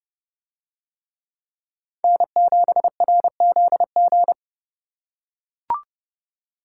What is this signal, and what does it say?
Morse code, heard as a keyed tone, about 28 wpm, 700 Hz. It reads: N7RZG